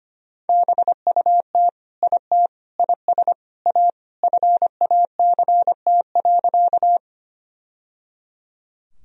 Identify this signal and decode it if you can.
Morse code, heard as a keyed tone, about 25 wpm, 700 Hz. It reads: BUT IT IS A FACT.
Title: Sentences from Top 1000 Words - Review Part 3 - 25wpm